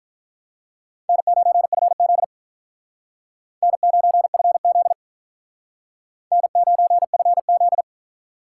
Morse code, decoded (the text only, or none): N9FZ N9FZ N9FZ